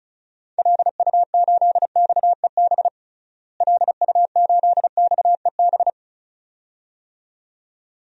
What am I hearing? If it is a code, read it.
Morse code, heard as a keyed tone, about 36 wpm, 700 Hz. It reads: LU8XEB LU8XEB
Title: DXCC-100 Call Signs - 6 Characters - Repeated 2x - 35wpm